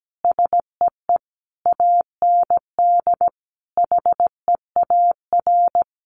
Morse code, heard as a keyed tone, about 17 wpm, 700 Hz. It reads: SEE AND HEAR
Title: Sentences from Top 1100 Words 17wpm